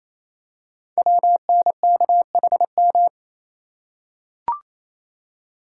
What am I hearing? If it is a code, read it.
Morse code, heard as a keyed tone, about 29 wpm, 700 Hz. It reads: WNKHM
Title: Code Groups - 5 Letters - 28wpm